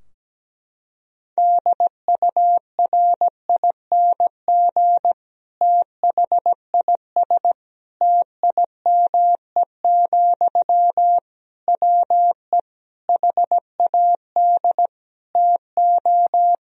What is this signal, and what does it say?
Morse code, heard as a keyed tone, about 17 wpm, 700 Hz. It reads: DURING THIS TIME, WE HAD TO
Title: Sentences from Top 1000 Words - Review Part 2 - 17wpm